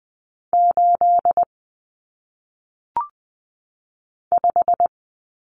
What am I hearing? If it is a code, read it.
Morse code, heard as a keyed tone, about 20 wpm, 700 Hz. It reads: 8 5
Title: Numbers - 1-Digit 20wpm